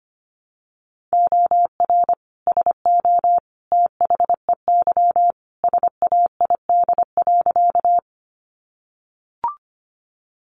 Morse code, heard as a keyed tone, about 25 wpm, 700 Hz. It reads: OR SO THEY SAID.
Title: Sentences from Top 100 Words 25wpm